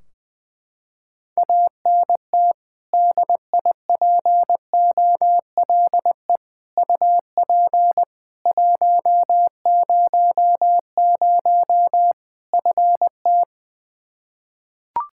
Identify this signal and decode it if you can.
Morse code, heard as a keyed tone, about 20 wpm, 700 Hz. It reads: ANT DIPOLE UP 100 FT
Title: Practice QSO - Exchange 2 - Info - 20wpm